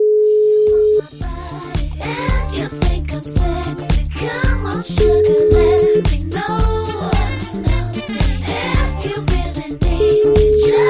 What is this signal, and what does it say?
Ringback tone (Europe, 425 Hz).